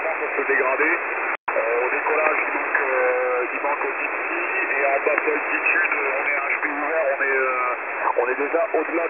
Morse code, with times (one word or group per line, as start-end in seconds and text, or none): none